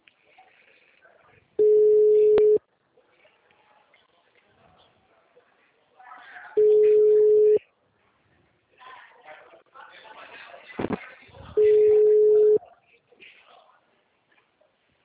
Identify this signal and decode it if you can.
Ringback tone (Europe, 425 Hz), with speech.